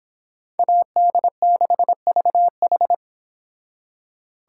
Morse code, heard as a keyed tone, about 31 wpm, 700 Hz. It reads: AD6VH